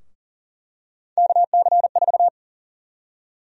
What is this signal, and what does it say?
Morse code, heard as a keyed tone, about 40 wpm, 700 Hz. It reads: KC4
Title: USA Call Sign Prefixes - 40wpm